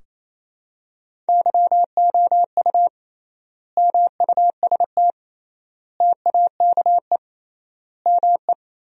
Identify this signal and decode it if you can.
Morse code, heard as a keyed tone, about 28 wpm, 700 Hz. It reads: YOU MUST TAKE ME
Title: 3x Word-Spacing - Sentences from Top 200 Words - 28wpm